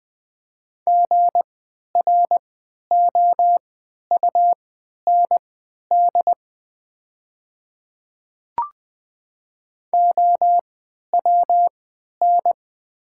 Morse code, heard as a keyed tone, about 20 wpm, spaced as for 11 wpm, 700 Hz. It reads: GROUND OWN
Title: 3x Character Spacing - Top 500 Words - 20wpm